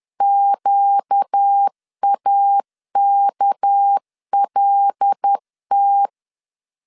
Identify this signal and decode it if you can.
Morse code, heard as a keyed tone, about 10 wpm, 800 Hz. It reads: QAKLT
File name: LEZIONI / 8 YJ45 / 8 Y 10wpm 2615